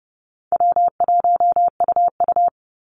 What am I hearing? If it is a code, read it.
Morse code, heard as a keyed tone, about 30 wpm, 700 Hz. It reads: W1UU
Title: Easy Call Signs - 30wpm